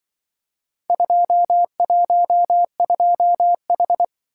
Morse code, heard as a keyed tone, about 24 wpm, 700 Hz. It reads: 212H
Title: Speed-Racing - Four Letter-Number Combinations - 20wpm